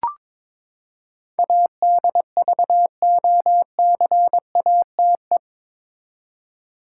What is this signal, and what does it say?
Morse code, heard as a keyed tone, about 22 wpm, 700 Hz. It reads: ADVOCATE
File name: practice / 22wpm / Eight Letter Words - 22wpm